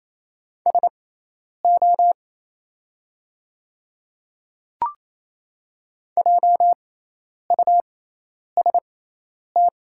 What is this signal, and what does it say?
Morse code, heard as a keyed tone, about 28 wpm, spaced as for 10 wpm, 700 Hz. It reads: SO JUST
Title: 6x Character Spacing - Top 200 Words - 28wpm